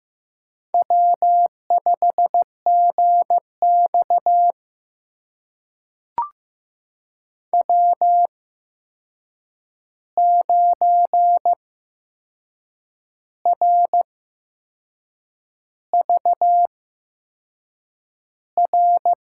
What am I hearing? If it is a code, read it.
Morse code, heard as a keyed tone, about 15 wpm, spaced as for 6 wpm, 700 Hz. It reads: W5GX W9RVR
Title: USA Call Signs - 8x Character Spacing - 15wpm